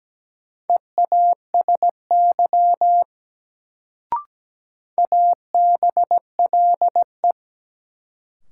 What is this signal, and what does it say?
Morse code, heard as a keyed tone, about 17 wpm, 700 Hz. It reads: EASY ABLE
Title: Sentences from Top 700 Words - New Word Drill - 17wpm